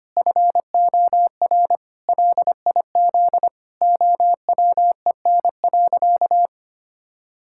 Morse code, heard as a keyed tone, about 25 wpm, 700 Hz. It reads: FOR LIZ OWEN.
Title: Pangram Challenge v2 - 25wpm